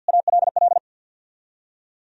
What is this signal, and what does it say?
Morse code, heard as a keyed tone, about 60 wpm, 700 Hz. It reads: ALL
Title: Speed-Racing - Sets of 2 Words 50wpm